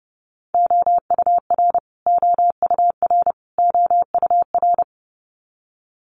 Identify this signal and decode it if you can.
Morse code, heard as a keyed tone, about 30 wpm, 700 Hz. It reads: OUR OUR OUR
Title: Top 200 Words Repeated 30wpm